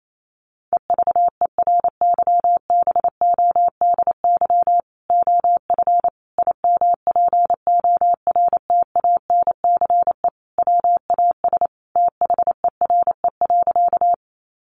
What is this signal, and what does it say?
Morse code, heard as a keyed tone, about 28 wpm, 700 Hz. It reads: EVERYBODY OF IMPORTANCE WAS THERE.
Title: Sentences from Top 1400 Words 28wpm